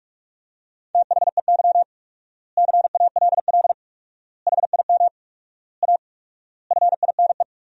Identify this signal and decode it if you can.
Morse code, heard as a keyed tone, about 46 wpm, 700 Hz. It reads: THEY CALL HIM A FINE
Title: 4x Word-Spacing - Sentences from Top 500 Words - 45wpm